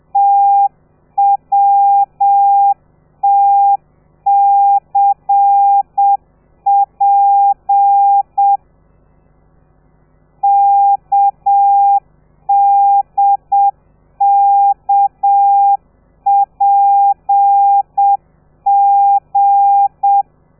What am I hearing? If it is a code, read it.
Morse code, heard as a keyed tone, about 7 wpm, 800 Hz. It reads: TWTCP KDKPG
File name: telegrafia / i2tfj / Lez13 30